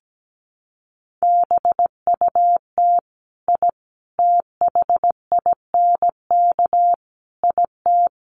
Morse code, heard as a keyed tone, about 17 wpm, 700 Hz. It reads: BUT I THINK IT
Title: Sets of 4 Words 17wpm